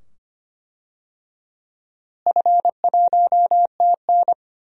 Morse code, heard as a keyed tone, about 25 wpm, 700 Hz. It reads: F1TN
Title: DXCC-100 Call Signs - 4 Characters - Repeated 2x - 25wpm